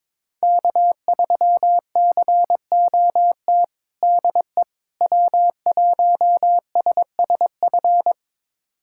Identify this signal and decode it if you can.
Morse code, heard as a keyed tone, about 22 wpm, 700 Hz. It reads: K3COT DE W1SSF
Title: USA Callsign DE Callsign 22wpm